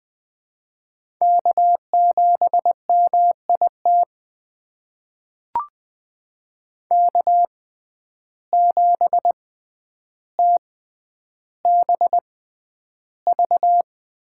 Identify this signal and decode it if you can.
Morse code, heard as a keyed tone, about 20 wpm, spaced as for 10 wpm, 700 Hz. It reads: K7MIT K7TBV
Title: USA Call Signs - 5 Characters - 6x Character Spacing - 20wpm